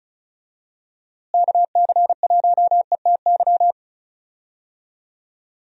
Morse code, heard as a keyed tone, about 36 wpm, 700 Hz. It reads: KC1ETY
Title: USA Call Signs - 6 Characters - Repeated 3x - 35wpm